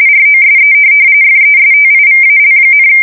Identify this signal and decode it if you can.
RTTY, about 45.45 baud. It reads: ROSE FESTIVAL,
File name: fsk170 / USS-New-Jersey